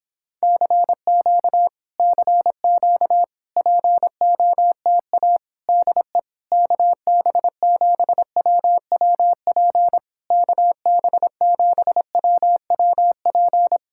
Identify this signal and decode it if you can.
Morse code, heard as a keyed tone, about 26 wpm, 700 Hz. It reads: CQ CQ POTA DE KB7WWP KB7WWP